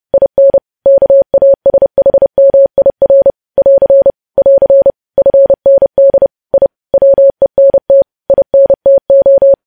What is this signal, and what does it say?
Morse code, heard as a keyed tone, about 30 wpm, 550 Hz. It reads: IN KASHMIR <AR> <AR> FND I WENT INTO
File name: podcast / News headlines-1x-30x30